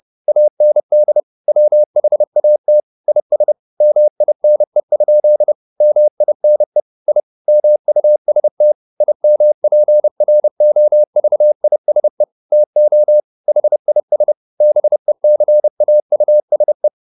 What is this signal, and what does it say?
Morse code, heard as a keyed tone, about 30 wpm, 600 Hz. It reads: AND WHAT IS MINE? MINE I MUST IMPROVISE TO HIS BECAUSE
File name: the old man and the sea / the old man and the sea 1 30wpm